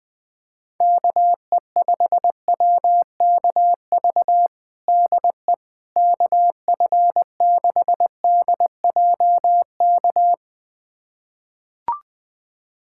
Morse code, heard as a keyed tone, about 21 wpm, 700 Hz. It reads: KE5WKV DE KF6DJK
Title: USA Callsign DE Callsign - 6 Characters - 20wpm